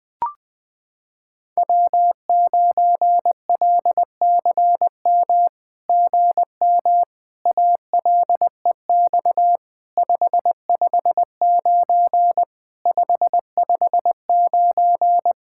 Morse code, heard as a keyed tone, about 20 wpm, 700 Hz. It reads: W9LCM GM ALEX 559 559
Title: SOTA QSO Practice 20wpm